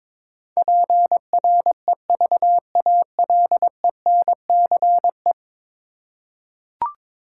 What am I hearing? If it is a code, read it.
Morse code, heard as a keyed tone, about 22 wpm, 700 Hz. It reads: PREVALENCE
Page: morse-code-ninja-practice-22wpm